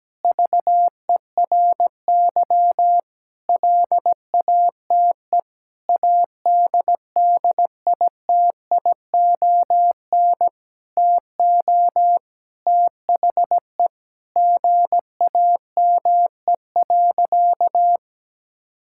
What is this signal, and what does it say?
Morse code, heard as a keyed tone, about 17 wpm, 700 Hz. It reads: VERY LATE ADDITION TO THE GAME.
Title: Sentences from Top 1000 Words - Review Part 6 - 17wpm